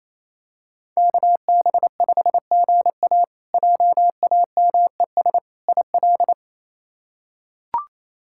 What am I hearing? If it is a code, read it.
Morse code, heard as a keyed tone, about 28 wpm, 700 Hz. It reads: KB5GA JAMES IL
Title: K1USN SST QSO Practice - 28wpm